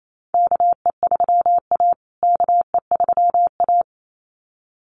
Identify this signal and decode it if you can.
Morse code, heard as a keyed tone, about 28 wpm, 700 Hz. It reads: KE3A KE3A
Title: USA Call Signs Repeated 2x 28wpm